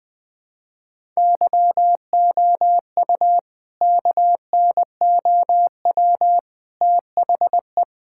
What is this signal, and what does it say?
Morse code, heard as a keyed tone, about 20 wpm, 700 Hz. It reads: YOU KNOW THE
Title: Sentences from Top 1000 Words - Review Part 5 - 20wpm